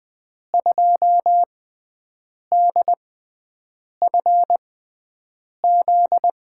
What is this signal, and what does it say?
Morse code, heard as a keyed tone, about 20 wpm, spaced as for 7 wpm, 700 Hz. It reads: 2DFZ
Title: USA Call Signs - 6 Characters - 6x Character Spacing - 20wpm